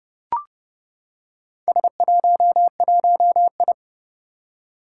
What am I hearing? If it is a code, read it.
Morse code, heard as a keyed tone, about 29 wpm, 700 Hz. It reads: S11I
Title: Four Challenging Letter-Number Combinations 30wpm